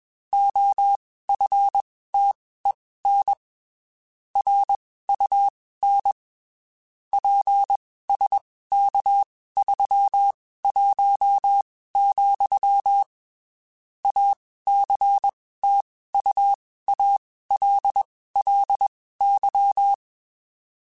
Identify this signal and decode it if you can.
Morse code, heard as a keyed tone, about 21 wpm, spaced as for 15 wpm, 800 Hz. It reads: OFTEN RUN PSK31, ACTUALLY
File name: (spaced) / QSOs Spaced-mp3s / kc7sac @20 sp